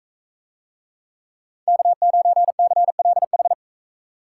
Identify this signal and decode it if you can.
Morse code, heard as a keyed tone, about 41 wpm, 700 Hz. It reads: K9CLH